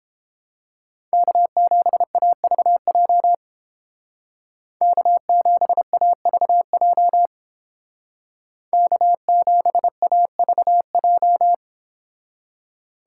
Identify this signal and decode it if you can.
Morse code, heard as a keyed tone, about 28 wpm, 700 Hz. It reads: K7AVJ K7AVJ K7AVJ